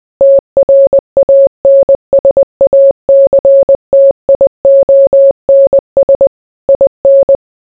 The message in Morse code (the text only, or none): TRANSACTIONS IN